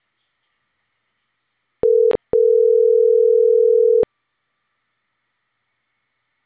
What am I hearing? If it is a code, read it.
Ringback tone (North America), 440 + 480 Hz.